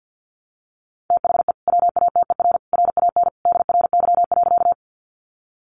Morse code, heard as a keyed tone, about 50 wpm, 700 Hz. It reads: THE WATER RAN DRY.
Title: Sentences from Top 500 Words 50wpm